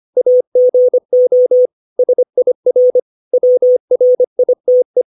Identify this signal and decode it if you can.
Morse code, heard as a keyed tone, about 25 wpm, 500 Hz. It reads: AGO SIR WRITE